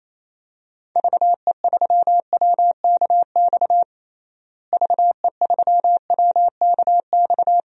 Morse code, heard as a keyed tone, about 28 wpm, 700 Hz. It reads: VE3WKX VE3WKX
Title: DXCC-100 Call Signs - 6 Characters - Repeated 3x - 28wpm